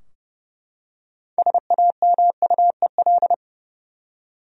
Morse code, heard as a keyed tone, about 30 wpm, 700 Hz. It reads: SAMUEL
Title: USA Common Names - 30wpm